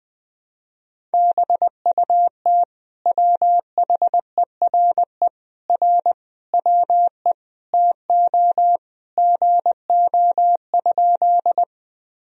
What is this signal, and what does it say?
Morse code, heard as a keyed tone, about 20 wpm, 700 Hz. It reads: BUT WHERE R WE TO GO?
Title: Sentences from Top 200 Words and Abbreviations 20wpm